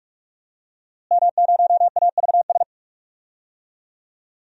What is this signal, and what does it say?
Morse code, heard as a keyed tone, about 46 wpm, 700 Hz. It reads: M0AVS
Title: DXCC-100 Call Signs - 5 Characters - Repeated 3x - 45wpm